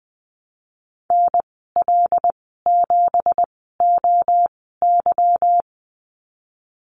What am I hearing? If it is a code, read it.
Morse code, heard as a keyed tone, about 20 wpm, spaced as for 15 wpm, 700 Hz. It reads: NL7OY
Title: 167 Lesson 27g - Number 7 - Call Signs 20wpm